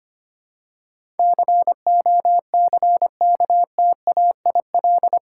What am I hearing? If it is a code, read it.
Morse code, heard as a keyed tone, about 25 wpm, 700 Hz. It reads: COCKTAIL
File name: Eight Letter Words - 25wpm